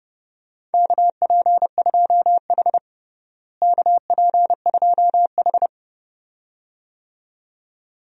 Morse code, heard as a keyed tone, about 32 wpm, 700 Hz. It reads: KP2H KP2H
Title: DXCC-100 Call Signs - 4 Characters - Repeated 2x - 30wpm